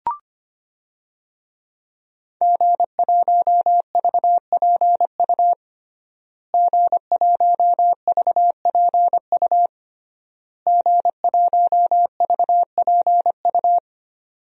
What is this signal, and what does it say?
Morse code, heard as a keyed tone, about 25 wpm, 700 Hz. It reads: G1VPU G1VPU G1VPU
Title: DXCC-100 Call Signs - 5 Characters - Repeated 3x - 25wpm